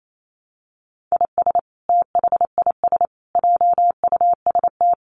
Morse code, heard as a keyed tone, about 28 wpm, 700 Hz. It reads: IS THIS JUST
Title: Sentences from Top 1100 Words 28wpm